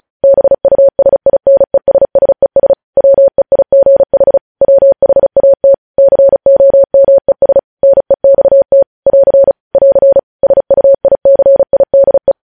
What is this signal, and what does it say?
Morse code, heard as a keyed tone, about 36 wpm, 550 Hz. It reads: BUSINESSES WEIGH WHAT COMES NEXT <AR> <AR> SUICIDE
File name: podcast / News headlines-1x-35x35